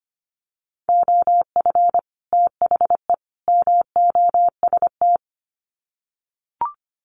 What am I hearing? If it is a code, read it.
Morse code, heard as a keyed tone, about 25 wpm, 700 Hz. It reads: OF THE MOST
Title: Sets of 3 Words 25wpm